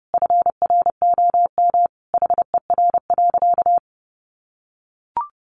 Morse code, heard as a keyed tone, about 30 wpm, 700 Hz. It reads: FROM HER.
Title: Sentences from Top 900 Words 30wpm